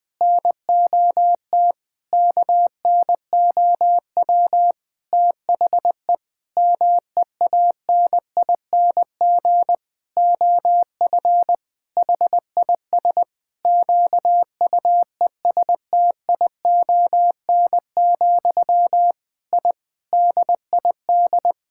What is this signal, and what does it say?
Morse code, heard as a keyed tone, about 20 wpm, 700 Hz. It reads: NOT KNOW THE MEANING OF HIS QUESTION, I DID
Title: Sentences from Top 1000 Words - Review Part 6 - 20wpm